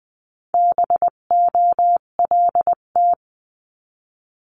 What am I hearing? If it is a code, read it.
Morse code, heard as a keyed tone, about 20 wpm, 700 Hz. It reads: BOLT
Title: 258 Lesson 42c - Words 16-20wpm